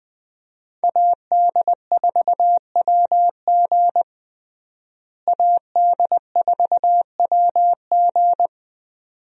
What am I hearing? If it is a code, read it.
Morse code, heard as a keyed tone, about 21 wpm, 700 Hz. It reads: AD4WG AD4WG
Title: USA Call Signs - 5 Characters - Repeated 2x - 20wpm